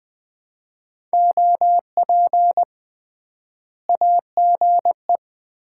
Morse code, heard as a keyed tone, about 21 wpm, 700 Hz. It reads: OP AGE
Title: Common QSO Phrases - 3x Word-Spacing - 20wpm